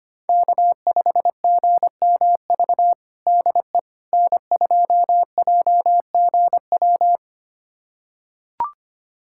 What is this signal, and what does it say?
Morse code, heard as a keyed tone, about 25 wpm, 700 Hz. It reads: K5GMV DE N2JGW
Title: USA Callsign DE Callsign - 5 Characters - 25wpm